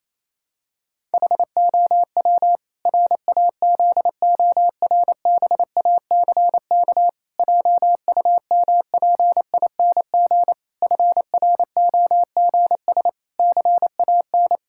Morse code, heard as a keyed tone, about 28 wpm, 700 Hz. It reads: HOW RAZORBACK JUMPING FROGS CAN
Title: Pangram Challenge v2 - 28wpm